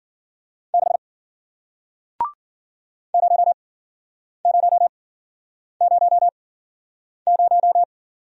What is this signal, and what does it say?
Morse code, heard as a keyed tone, about 54 wpm, spaced as for 9 wpm, 700 Hz. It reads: 6 0000